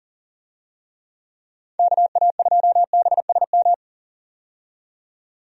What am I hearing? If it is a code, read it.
Morse code, heard as a keyed tone, about 40 wpm, 700 Hz. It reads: KA2BSM
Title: USA Call Signs - 6 Characters - 40wpm